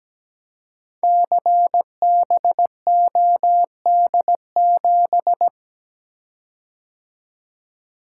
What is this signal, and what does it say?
Morse code, heard as a keyed tone, about 17 wpm, 700 Hz. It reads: CBOD7
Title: Code Groups - 5 Letter:Numbers - 17wpm